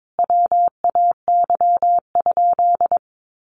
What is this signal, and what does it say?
Morse code, heard as a keyed tone, about 22 wpm, 700 Hz. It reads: WAY?
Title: Sentences from Top 1100 Words 22wpm